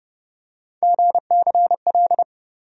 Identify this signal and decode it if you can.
Morse code, heard as a keyed tone, about 32 wpm, 700 Hz. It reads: GCL
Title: USA Call Sign Postfixes - 3 Letters 30wpm